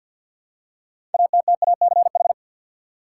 Morse code, heard as a keyed tone, about 50 wpm, 700 Hz. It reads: ATTACH
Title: Six Letter Words - 50wpm